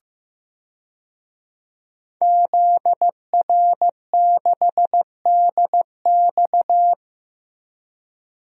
Morse code, heard as a keyed tone, about 15 wpm, 700 Hz. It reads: ZR6DX
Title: DXCC-100 Call Signs - 5 Characters - 15wpm